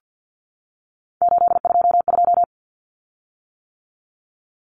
Morse code, heard as a keyed tone, about 50 wpm, 700 Hz. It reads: 822
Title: Common Cut Numbers - 3-Digit 50wpm